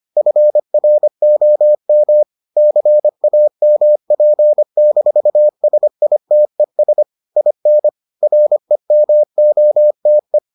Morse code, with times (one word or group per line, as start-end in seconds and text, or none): 0.2-2.2 FROM
2.6-7.0 CAMP-SITES
7.4-7.9 IN
8.2-10.4 REMOTE